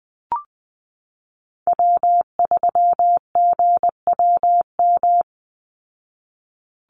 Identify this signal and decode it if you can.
Morse code, heard as a keyed tone, about 20 wpm, 700 Hz. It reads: W3GWM
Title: Call Signs Repeated 2x - 20wpm